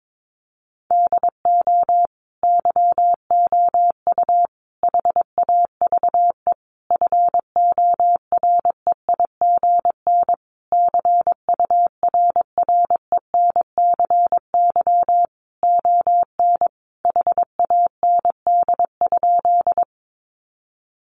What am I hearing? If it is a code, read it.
Morse code, heard as a keyed tone, about 22 wpm, 700 Hz. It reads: DO YOU HAVE FOREIGN CURRENCY ON HAND?
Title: Sentences from Top 1500 Words 22wpm